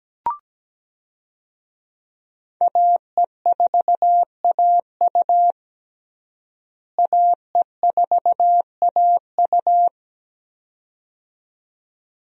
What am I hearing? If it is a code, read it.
Morse code, heard as a keyed tone, about 17 wpm, 700 Hz. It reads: AE4AU AE4AU
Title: USA Call Signs - 5 Characters - Repeated 2x - 17wpm